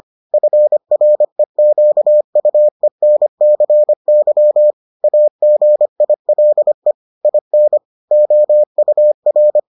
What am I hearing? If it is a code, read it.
Morse code, heard as a keyed tone, about 25 wpm, 600 Hz. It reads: FREQUENCY AGILE IN OUR